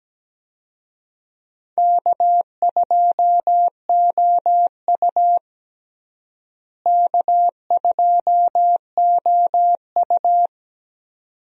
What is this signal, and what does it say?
Morse code, heard as a keyed tone, about 17 wpm, 700 Hz. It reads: K2OU K2OU
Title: USA Call Signs - 4 Characters - Repeated 2x - 17wpm